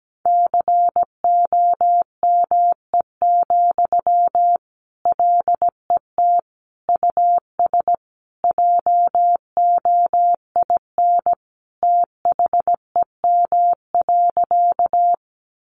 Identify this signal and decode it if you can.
Morse code, heard as a keyed tone, about 17 wpm, 700 Hz. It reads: COME, LET US JOIN THEM.
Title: Sentences from Top 1400 Words 17wpm